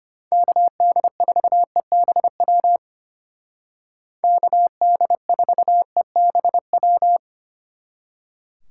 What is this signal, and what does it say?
Morse code, heard as a keyed tone, about 32 wpm, 700 Hz. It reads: KD4EBW KD4EBW